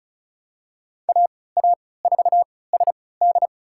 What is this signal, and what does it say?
Morse code, heard as a keyed tone, about 36 wpm, spaced as for 20 wpm, 700 Hz. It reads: AA4SD
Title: USA Call Signs - 5 Characters - 3x Character Spacing - 35wpm